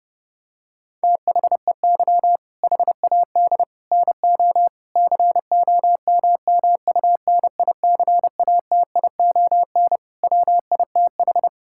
Morse code, heard as a keyed tone, about 30 wpm, 700 Hz. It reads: THEY HAD NO COMMUNICATION WITH
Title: Sentences from Top 1000 Words - Review Part 6 - 30wpm